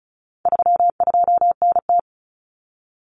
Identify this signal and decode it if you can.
Morse code, heard as a keyed tone, about 34 wpm, 700 Hz. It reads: 32NT
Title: Common Cut Numbers - 4-Digit 35wpm